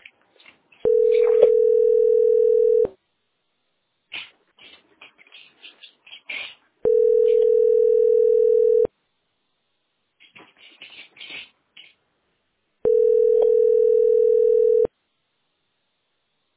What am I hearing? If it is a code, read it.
Ringback tone (North America), 440 + 480 Hz.